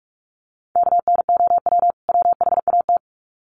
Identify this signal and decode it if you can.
Morse code, heard as a keyed tone, about 47 wpm, 700 Hz. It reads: KNOW WHAT
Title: Sets of 2 Words 45wpm